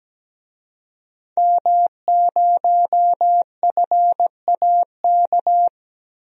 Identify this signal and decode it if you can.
Morse code, heard as a keyed tone, about 17 wpm, 700 Hz. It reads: M0FAK
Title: DXCC-100 Call Signs - 5 Characters - Repeated 3x - 17wpm